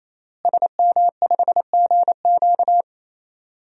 Morse code, heard as a keyed tone, about 28 wpm, 700 Hz. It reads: SM5GQ
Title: DXCC-100 Call Signs - 5 Characters - Repeated 3x - 28wpm